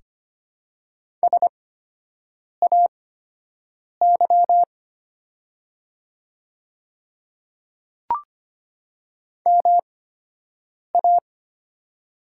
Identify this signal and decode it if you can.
Morse code, heard as a keyed tone, about 25 wpm, spaced as for 7 wpm, 700 Hz. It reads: SAY MA
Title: 8x Character Spacing - Top 500 Words - 25wpm